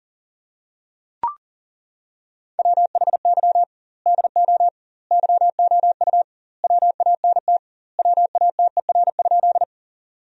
Morse code, heard as a keyed tone, about 37 wpm, 700 Hz. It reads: WHY DO YOU WANT WATER?
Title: 2x Word-Spacing - Sentences from Top 200 Words - 40wpm